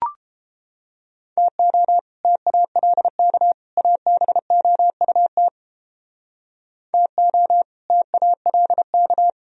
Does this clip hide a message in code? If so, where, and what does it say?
Yes: Morse code from 1.4 to 9.3 s, heard as a keyed tone, about 32 wpm, 700 Hz. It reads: TO TALK ABOUT TO TALK